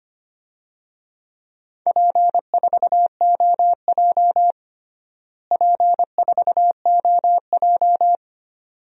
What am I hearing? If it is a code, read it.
Morse code, heard as a keyed tone, about 25 wpm, 700 Hz. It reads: P4OJ P4OJ
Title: DXCC-100 Call Signs - 4 Characters - Repeated 2x - 25wpm